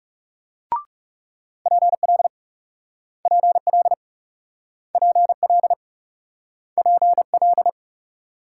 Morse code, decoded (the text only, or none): PL PL PL PL